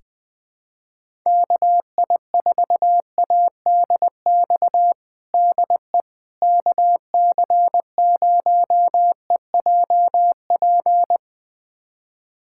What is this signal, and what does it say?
Morse code, heard as a keyed tone, about 20 wpm, 700 Hz. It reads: KI4ADX DE KC0EJP
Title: USA Callsign DE Callsign - 6 Characters - 20wpm